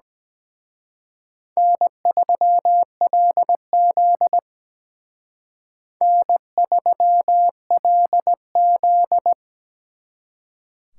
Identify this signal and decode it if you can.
Morse code, heard as a keyed tone, about 20 wpm, 700 Hz. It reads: N3LZ N3LZ
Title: Speed Racing - USA Call Signs 17wpm